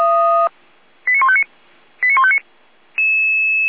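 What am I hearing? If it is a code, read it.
Five-tone selective call (ZVEI), heard as quick runs of five tones, tones of about 70 ms. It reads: 89159 89159
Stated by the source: ZVEI five-tone call received by SDR, from the server of Feuerwehr Orlamünde (fire brigade).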